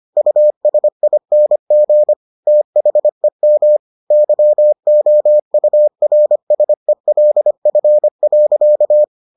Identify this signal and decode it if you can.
Morse code, heard as a keyed tone, about 25 wpm, 600 Hz. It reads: USING THEM YOURSELF.